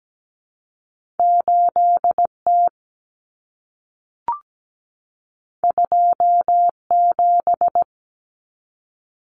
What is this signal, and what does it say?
Morse code, heard as a keyed tone, about 17 wpm, 700 Hz. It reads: 8T 27
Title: Common Cut Numbers - 2-Digit 17wpm